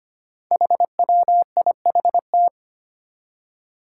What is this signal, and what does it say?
Morse code, heard as a keyed tone, about 25 wpm, 700 Hz. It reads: HWIHT
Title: Five Challenging Letter-Number Combinations 25wpm